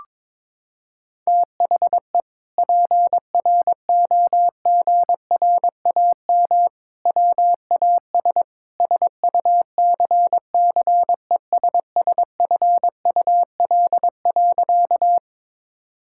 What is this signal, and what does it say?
Morse code, heard as a keyed tone, about 22 wpm, 700 Hz. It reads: THE PROGRAM WAS SUCCESSFUL.
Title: Sentences from Top 1000 Words - Review Part 5 - 22wpm